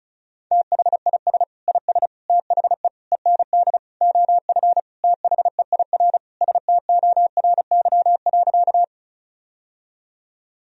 Morse code, heard as a keyed tone, about 34 wpm, 700 Hz. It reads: THIS IS THE END OF THEIR STORY.
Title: Sentences from Top 1000 Words - Review Part 1 - 35wpm